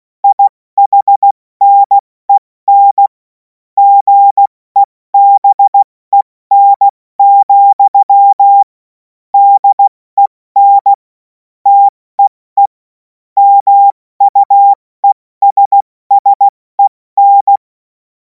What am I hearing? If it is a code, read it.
Morse code, heard as a keyed tone, about 16 wpm, 800 Hz. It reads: IHNEN GEBEN, DEN TEE MUESSEN